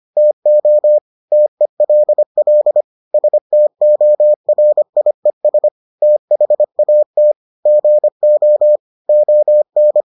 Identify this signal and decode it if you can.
Morse code, heard as a keyed tone, about 25 wpm, 600 Hz. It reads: TO TELL STORIES THAT GO ON